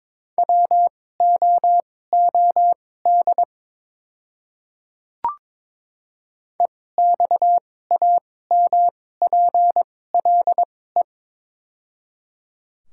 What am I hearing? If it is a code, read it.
Morse code, heard as a keyed tone, about 21 wpm, spaced as for 16 wpm, 700 Hz. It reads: WOOD EXAMPLE
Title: 2x Character Spacing - Top 300 Words - 22wpm